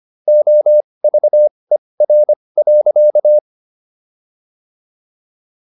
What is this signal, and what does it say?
Morse code, heard as a keyed tone, about 25 wpm, spaced as for 20 wpm, 600 Hz. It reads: OVER.